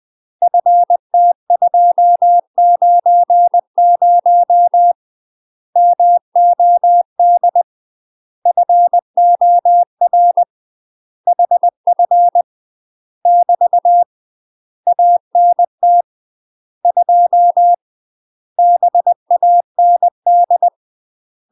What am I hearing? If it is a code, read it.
Morse code, heard as a keyed tone, about 20 wpm, 700 Hz. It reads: FT290 MOD FOR HF <BT> ANT 2 BAND
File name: UK MASTER QSOs TRAINING / 20WPM / UK MASTER QSOs 0022